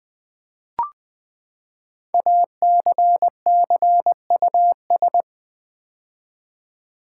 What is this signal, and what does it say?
Morse code, heard as a keyed tone, about 20 wpm, 700 Hz. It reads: ACCUS
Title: Build A Word - 5 Letters At A Time - 10-Letter Words - 20wpm